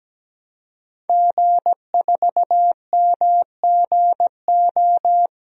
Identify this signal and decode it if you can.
Morse code, heard as a keyed tone, about 17 wpm, 700 Hz. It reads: G4MGO
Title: DXCC-100 Call Signs - 5 Characters - Repeated 2x - 17wpm